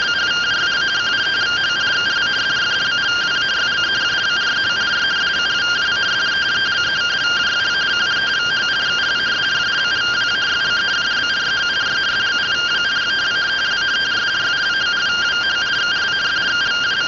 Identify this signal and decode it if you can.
Two-tone teleprinter signal, about 170 Hz shift.
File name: findings / SAGITAL nb203 fr61 3mm.gen-07 rtty rx